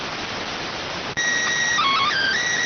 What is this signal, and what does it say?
SSTV (slow-scan television), PD 290.